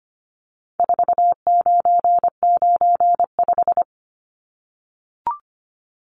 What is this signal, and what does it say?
Morse code, heard as a keyed tone, about 25 wpm, 700 Hz. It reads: 4995
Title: Numbers - 4-Digit 25wpm